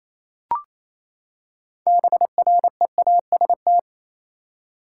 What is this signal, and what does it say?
Morse code, heard as a keyed tone, about 28 wpm, 700 Hz. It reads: BREAST
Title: Six Letter Words - 28wpm